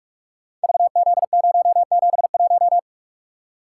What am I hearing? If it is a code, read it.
Morse code, heard as a keyed tone, about 44 wpm, 690 Hz. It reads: VZ071
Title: Code Groups - 5 Letter:Numbers - 45wpm